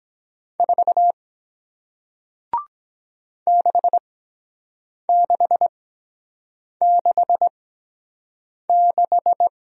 Morse code, heard as a keyed tone, about 23 wpm, spaced as for 7 wpm, 700 Hz. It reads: 4 6666